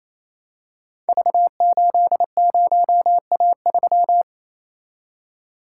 Morse code, heard as a keyed tone, about 27 wpm, 700 Hz. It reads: V80A3
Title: Code Groups - 5 Letter:Numbers - 28wpm